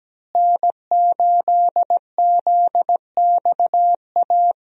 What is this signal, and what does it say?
Morse code, heard as a keyed tone, about 17 wpm, 700 Hz. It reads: N8ZXA
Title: USA Call Signs - 5 Characters - Repeated 3x - 17wpm